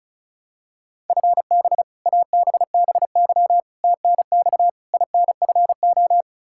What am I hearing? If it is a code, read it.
Morse code, heard as a keyed tone, about 36 wpm, 700 Hz. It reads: FB ABBY TNX INFO
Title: Practice QSO - Exchange 3 - More Info - 35wpm